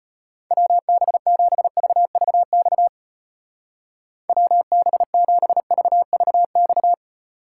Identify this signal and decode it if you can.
Morse code, heard as a keyed tone, about 32 wpm, 700 Hz. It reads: WB7VVX WB7VVX